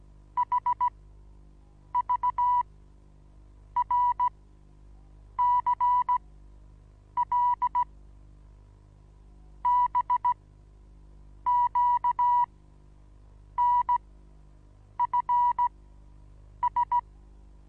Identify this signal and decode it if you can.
Morse code, heard as a keyed tone, about 17 wpm, spaced as for 7 wpm, 980 Hz. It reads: HVRCL BQNFS
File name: Onlinelehrgang / morsen / DJ4UF Morsen25